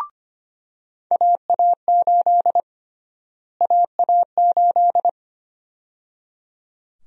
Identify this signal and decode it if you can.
Morse code, heard as a keyed tone, about 25 wpm, 700 Hz. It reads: AA8 AA8
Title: USA Call Sign Prefixes - Repeated 2x - 25wpm